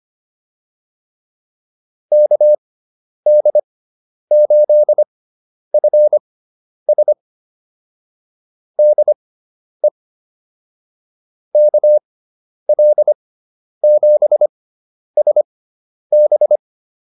Morse code, heard as a keyed tone, about 25 wpm, spaced as for 10 wpm, 600 Hz. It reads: KD8FS DE KL7SB
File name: QSO107 10